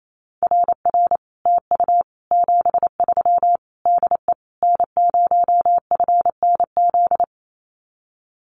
Morse code, heard as a keyed tone, about 28 wpm, 700 Hz. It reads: RR TU 73 DE N0FNZ
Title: Common QSO Phrases Encore 28wpm